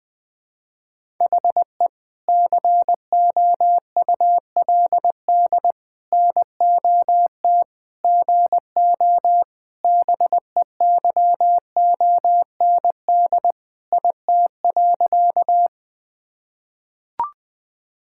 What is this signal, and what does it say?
Morse code, heard as a keyed tone, about 21 wpm, 700 Hz. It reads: HE COULD NOT GO BEYOND IT.
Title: Sentences from Top 1000 Words - Review Part 6 - 20wpm